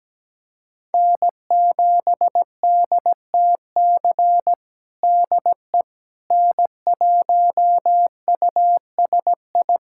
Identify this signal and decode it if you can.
Morse code, heard as a keyed tone, about 17 wpm, 700 Hz. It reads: N7DTC DE N1USI
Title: USA Callsign DE Callsign - 5 Characters - 17wpm